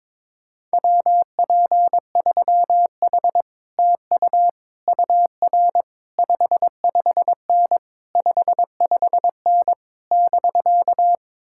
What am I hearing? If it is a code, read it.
Morse code, heard as a keyed tone, about 22 wpm, 700 Hz. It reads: WP3H TU UR 55N 55N <BK>
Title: Speed-Racing Common QSO Phrases Encore - 22wpm